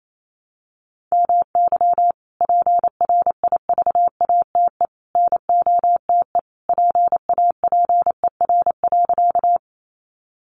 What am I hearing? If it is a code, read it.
Morse code, heard as a keyed tone, about 28 wpm, 700 Hz. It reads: MY PRIVATE NOTE PAPER.
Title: Sentences from Top 1000 Words 28wpm